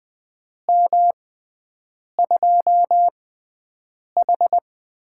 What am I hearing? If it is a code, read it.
Morse code, heard as a keyed tone, about 20 wpm, spaced as for 7 wpm, 700 Hz. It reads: M2H
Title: USA Call Signs - 4 Characters - 6x Character Spacing - 20wpm